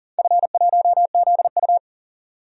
Morse code, heard as a keyed tone, about 38 wpm, 700 Hz. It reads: F1ZU
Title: DXCC-100 Call Signs - 4 Characters - 40wpm